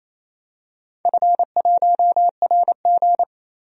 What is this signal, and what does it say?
Morse code, heard as a keyed tone, about 29 wpm, 700 Hz. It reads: F1RG
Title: DXCC-100 Call Signs - 4 Characters - 28wpm